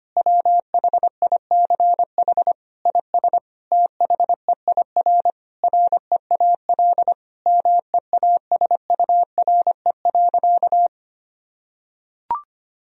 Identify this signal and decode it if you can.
Morse code, heard as a keyed tone, about 25 wpm, 700 Hz. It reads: WHICH IS THEIR REAL MEASURE.
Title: Sentences from Top 1000 Words - Review Part 1 - 25wpm